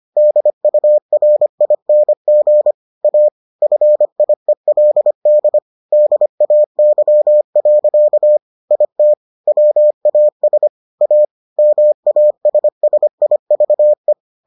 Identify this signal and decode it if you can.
Morse code, heard as a keyed tone, about 25 wpm, 600 Hz. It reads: DURING A FIELD DAY. IT WAS A MASSIVE